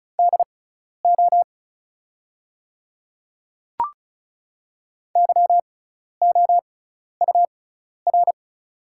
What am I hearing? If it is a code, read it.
Morse code, heard as a keyed tone, about 36 wpm, spaced as for 12 wpm, 700 Hz. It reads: DO YOUR
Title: 6x Character Spacing - Top 100 Words - 35wpm